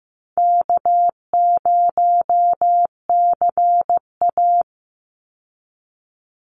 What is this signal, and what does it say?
Morse code, heard as a keyed tone, about 15 wpm, 700 Hz. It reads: K0CA
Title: Easy Call Signs - 15wpm